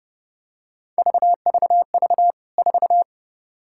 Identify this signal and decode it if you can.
Morse code, heard as a keyed tone, about 30 wpm, 700 Hz. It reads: VVV 4
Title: K1USN SST QSO Practice 30wpm